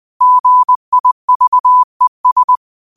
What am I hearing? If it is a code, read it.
Morse code, heard as a keyed tone, about 20 wpm, 1000 Hz. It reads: GIVES
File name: qotdfiles / 20wpm